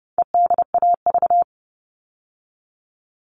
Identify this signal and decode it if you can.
Morse code, heard as a keyed tone, about 30 wpm, 700 Hz. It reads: EDAV
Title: All Cut Numbers - 4-Digit 30wpm